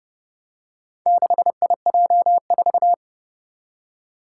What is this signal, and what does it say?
Morse code, heard as a keyed tone, about 30 wpm, 700 Hz. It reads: 6IJ4
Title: Four Challenging Letter-Number Combinations 30wpm